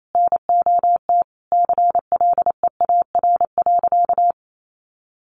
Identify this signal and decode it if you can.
Morse code, heard as a keyed tone, about 28 wpm, 700 Hz. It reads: NOT CLEAR.
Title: Sentences from Top 1300 Words 28wpm